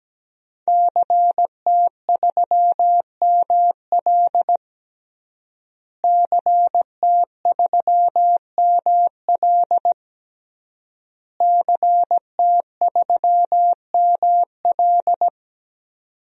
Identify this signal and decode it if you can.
Morse code, heard as a keyed tone, about 17 wpm, 700 Hz. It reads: CT3ML CT3ML CT3ML
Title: DXCC-100 Call Signs - 5 Characters - Repeated 3x - 17wpm